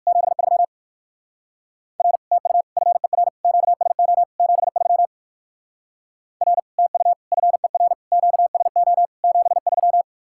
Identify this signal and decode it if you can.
Morse code, heard as a keyed tone, about 58 wpm, 700 Hz. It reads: 73 R TU FER QSO 73 R TU FER QSO 73